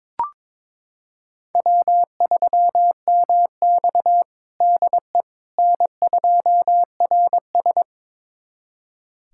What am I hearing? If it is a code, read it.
Morse code, heard as a keyed tone, about 22 wpm, 700 Hz. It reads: W3MX DE N2RS
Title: USA Callsign DE Callsign - 4 Characters - 22wpm